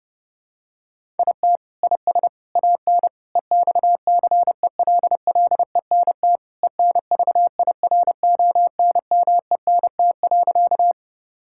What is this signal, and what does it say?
Morse code, heard as a keyed tone, about 30 wpm, 700 Hz. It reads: IT IS AN EXCELLENT ENVIRONMENT.
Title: Sentences from Top 1000 Words - Review Part 5 - 30wpm